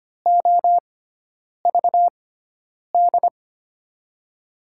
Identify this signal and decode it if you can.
Morse code, heard as a keyed tone, about 25 wpm, spaced as for 9 wpm, 700 Hz. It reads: OVD
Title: USA Call Signs - 6 Characters - 6x Character Spacing - 25wpm